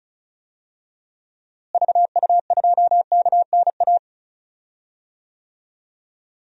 Morse code, heard as a keyed tone, about 36 wpm, 700 Hz. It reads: VU2KNA
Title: DXCC-100 Call Signs - 6 Characters - Repeated 2x - 35wpm